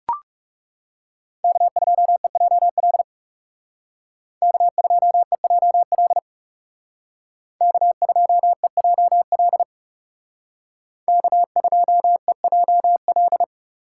Morse code, heard as a keyed tone, about 36 wpm, 700 Hz. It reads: K2EJL K2EJL K2EJL K2EJL
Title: Speed Racing - USA 1x3 Call Signs 30wpm